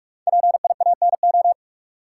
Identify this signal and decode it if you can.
Morse code, heard as a keyed tone, about 44 wpm, 700 Hz. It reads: PIANO
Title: Five Letter Words - 45wpm